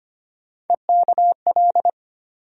Morse code, heard as a keyed tone, about 25 wpm, 700 Hz. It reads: EKL
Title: USA Call Sign Postfixes - 3 Letters 25wpm